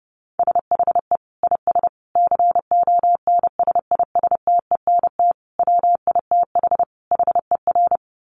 Morse code, heard as a keyed tone, about 30 wpm, 700 Hz. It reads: SHE IS CONSISTENT WITH HER
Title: Sentences from Top 1500 Words 30wpm